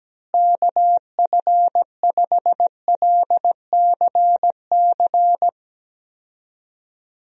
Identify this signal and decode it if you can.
Morse code, heard as a keyed tone, about 17 wpm, 700 Hz. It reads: KF5LCC
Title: USA Call Signs - 6 Characters - Repeated 2x - 17wpm